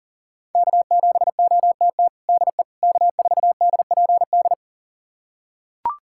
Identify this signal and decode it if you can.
Morse code, heard as a keyed tone, about 40 wpm, 700 Hz. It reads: K7OTT DE K4DPD
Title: USA Callsign DE Callsign - 5 Characters - 40wpm